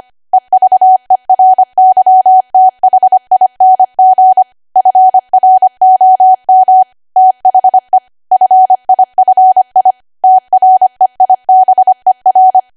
Morse code, heard as a keyed tone, about 25 wpm, 750 Hz. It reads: EVERYTHING FROM THE FIFI TREIBER
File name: cwa / W1AW-Level3 / 25WPM